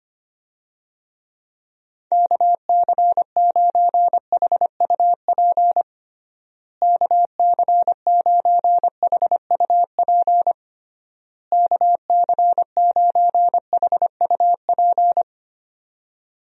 Morse code, heard as a keyed tone, about 25 wpm, 700 Hz. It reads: KC9HUP KC9HUP KC9HUP
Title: USA Call Signs - 6 Characters - Repeated 3x - 25wpm